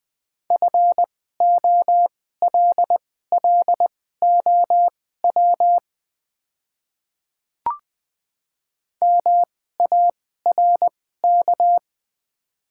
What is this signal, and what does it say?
Morse code, heard as a keyed tone, about 21 wpm, spaced as for 15 wpm, 700 Hz. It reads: FOLLOW MARK
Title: 2x Character Spacing - Top 300 Words - 20wpm